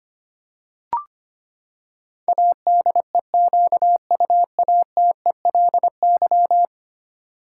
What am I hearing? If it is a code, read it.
Morse code, heard as a keyed tone, about 25 wpm, 700 Hz. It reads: ADEQUATELY
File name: Ten Letter Words - 25wpm